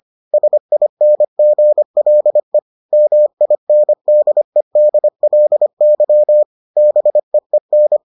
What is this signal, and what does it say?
Morse code, heard as a keyed tone, about 25 wpm, 600 Hz. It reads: SINGLE MINDEDLY BEEN